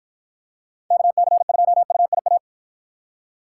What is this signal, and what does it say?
Morse code, heard as a keyed tone, about 50 wpm, 700 Hz. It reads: KC2UIA